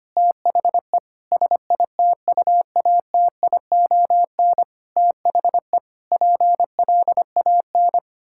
Morse code, heard as a keyed tone, about 25 wpm, 700 Hz. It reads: THE SITUATION THE PLAN
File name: practice / 25wpm / Sentences with 9-Word Phrases - 25wpm